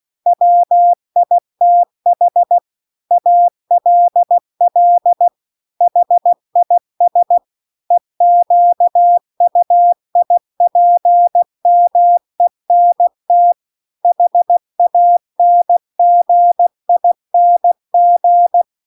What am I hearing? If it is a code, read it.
Morse code, heard as a keyed tone, about 16 wpm, 700 Hz. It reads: WITH ALL HIS EQUIPMENT HANGING